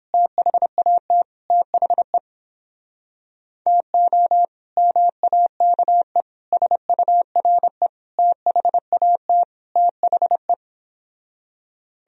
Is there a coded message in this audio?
Yes: Morse code from 0.1 to 10.5 s, heard as a keyed tone, about 29 wpm, 700 Hz. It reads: THAT THE TO MAKE SURE THAT THE